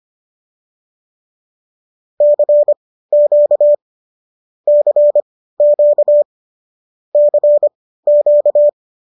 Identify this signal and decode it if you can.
Morse code, heard as a keyed tone, about 25 wpm, spaced as for 15 wpm, 590 Hz. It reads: CQ CQ CQ
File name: qso213 15